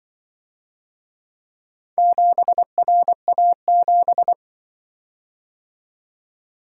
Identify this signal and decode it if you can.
Morse code, heard as a keyed tone, about 23 wpm, 700 Hz. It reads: 7RA7
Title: Speed-Racing - Four Letter-Number Combinations - 20wpm